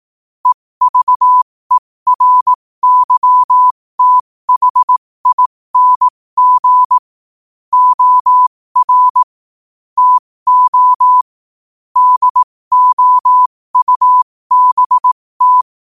Morse code, heard as a keyed tone, about 18 wpm, spaced as for 15 wpm, 1000 Hz. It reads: EVERYTHING OR TO DOUBT
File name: qotdfiles / 15wpm